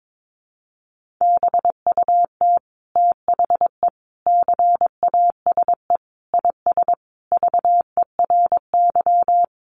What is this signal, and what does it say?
Morse code, heard as a keyed tone, about 22 wpm, 700 Hz. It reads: BUT THE CASE IS VERY
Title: Sentences from Top 700 Words 22wpm